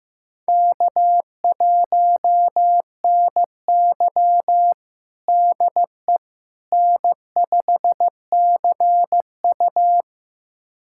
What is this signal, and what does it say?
Morse code, heard as a keyed tone, about 16 wpm, 700 Hz. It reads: K1NY DE N5CU
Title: USA Callsign DE Callsign - 4 Characters - 15wpm